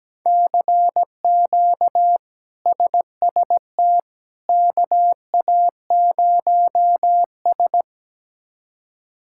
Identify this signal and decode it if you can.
Morse code, heard as a keyed tone, about 17 wpm, 700 Hz. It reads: CQ SST KA0S
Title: K1USN SST QSO Practice - 17wpm